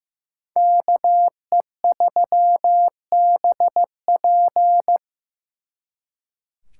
Morse code, heard as a keyed tone, about 18 wpm, 700 Hz. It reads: KE3BP
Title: Speed Racing - USA 2x2 Call Signs 15wpm